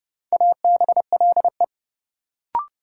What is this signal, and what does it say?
Morse code, heard as a keyed tone, about 29 wpm, 700 Hz. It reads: ABLE
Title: Sentences from Top 700 Words - New Word Drill - 30wpm